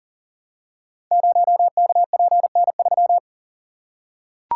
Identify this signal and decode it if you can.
Morse code, heard as a keyed tone, about 40 wpm, 690 Hz. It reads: 0KPN3
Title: Code Groups - 5 Letter:Numbers - 40wpm